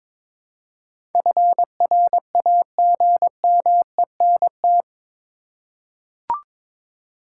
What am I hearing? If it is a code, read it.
Morse code, heard as a keyed tone, about 22 wpm, 700 Hz. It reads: FRAGMENT
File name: practice / 22wpm / Eight Letter Words - 22wpm